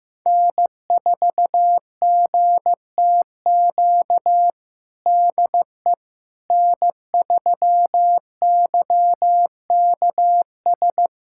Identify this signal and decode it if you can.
Morse code, heard as a keyed tone, about 15 wpm, 700 Hz. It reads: N4GTQ DE N3YKS
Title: USA Callsign DE Callsign - 5 Characters - 15wpm